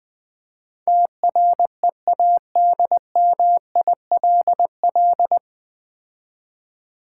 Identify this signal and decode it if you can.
Morse code, heard as a keyed tone, about 20 wpm, 700 Hz. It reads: TREADMILL
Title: Difficult Words - Speed-Racing with Hints - 20wpm